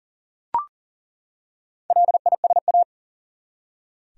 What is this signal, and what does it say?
Morse code, heard as a keyed tone, about 38 wpm, 700 Hz. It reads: LISA
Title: USA Common Names - Max 4 Letters - 40wpm